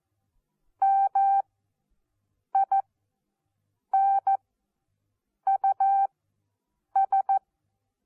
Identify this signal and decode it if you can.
Morse code, heard as a keyed tone, about 14 wpm, spaced as for 6 wpm, 780 Hz. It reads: MINUS
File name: Onlinelehrgang / morsen / DJ4UF Morsen14